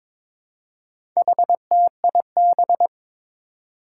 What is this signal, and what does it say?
Morse code, heard as a keyed tone, about 22 wpm, 700 Hz. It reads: HTIB
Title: Four Challenging Letter-Number Combinations 22wpm